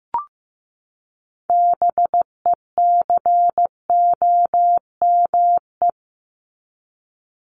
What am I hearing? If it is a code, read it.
Morse code, heard as a keyed tone, about 15 wpm, 700 Hz. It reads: BECOME
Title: Top 501-750 Words 15wpm